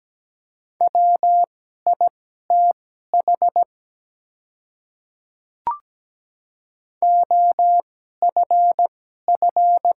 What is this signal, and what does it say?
Morse code, heard as a keyed tone, about 17 wpm, spaced as for 12 wpm, 700 Hz. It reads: WITH OFF
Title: 2x Character Spacing - Top 300 Words - 17wpm